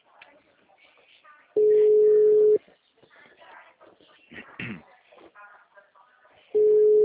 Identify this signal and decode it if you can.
Ringback tone (Europe, 425 Hz), with speech.